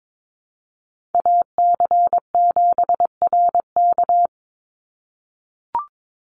Morse code, heard as a keyed tone, about 22 wpm, 700 Hz. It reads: AC7RK
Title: USA Call Signs 22wpm